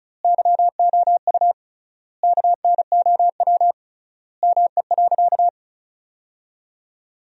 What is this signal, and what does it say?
Morse code, heard as a keyed tone, about 36 wpm, 700 Hz. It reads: YOU KNOW ME.
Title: 3x Word-Spacing - Sentences from Top 200 Words - 35wpm